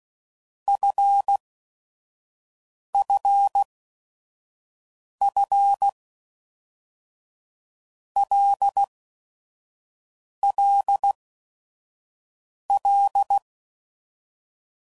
Morse code, heard as a keyed tone, about 16 wpm, 800 Hz. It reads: F F F L L L
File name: Basic Letter Groups By7Groups / LFCYQ / LFCYQ@5 f15